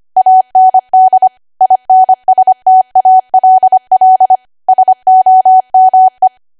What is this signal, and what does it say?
Morse code, heard as a keyed tone, about 25 wpm, 750 Hz. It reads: AND INSTALL SOME
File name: cwa / W1AW-Level3 / 25WPM